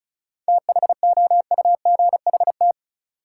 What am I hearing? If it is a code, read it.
Morse code, heard as a keyed tone, about 36 wpm, 700 Hz. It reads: THOUGHT
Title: Sentences from Top 300 Words - New Word Drill - 35wpm